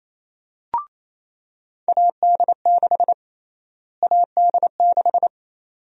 Morse code, heard as a keyed tone, about 28 wpm, 700 Hz. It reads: AD6 AD6
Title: USA Call Sign Prefixes - Repeated 2x - 28wpm